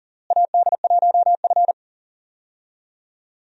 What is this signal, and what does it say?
Morse code, heard as a keyed tone, about 40 wpm, 700 Hz. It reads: AD1F
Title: USA Call Signs - 4 Characters - Repeated 3x - 40wpm